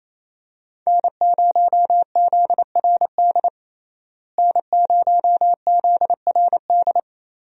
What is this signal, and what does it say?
Morse code, heard as a keyed tone, about 28 wpm, 700 Hz. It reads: N0ZRD N0ZRD
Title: USA Call Signs - 5 Characters - Repeated 2x - 28wpm